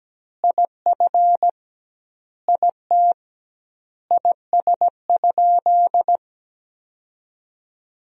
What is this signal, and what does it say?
Morse code, heard as a keyed tone, about 17 wpm, 700 Hz. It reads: IF IT IS?
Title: 2x Word-Spacing - Sentences from Top 100 Words - 17wpm